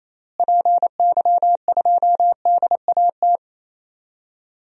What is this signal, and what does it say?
Morse code, heard as a keyed tone, about 28 wpm, 700 Hz. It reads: PY2DAT
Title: DXCC-100 Call Signs - 6 Characters - Repeated 3x - 28wpm